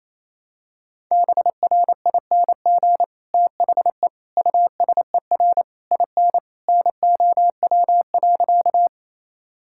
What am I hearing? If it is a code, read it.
Morse code, heard as a keyed tone, about 27 wpm, 700 Hz. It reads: BRING THE USER IN NOW.
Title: Sentences from Top 1000 Words - Review Part 5 - 28wpm